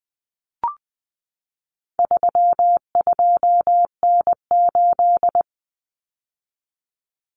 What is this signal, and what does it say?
Morse code, heard as a keyed tone, about 20 wpm, 700 Hz. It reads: 32N8
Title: Common Cut Numbers - 4-Digit 20wpm